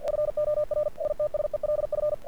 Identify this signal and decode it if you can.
Morse code, heard as a keyed tone, about 47 wpm, 590 Hz. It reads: FOR ITSELF
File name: archived / CRSnet